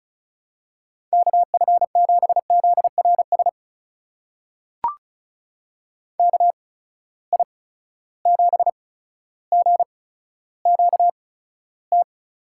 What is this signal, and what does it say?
Morse code, heard as a keyed tone, about 34 wpm, spaced as for 15 wpm, 700 Hz. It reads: KF7ZRS KI7GQT
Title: USA Call Signs - 6 Characters - 8x Character Spacing - 35wpm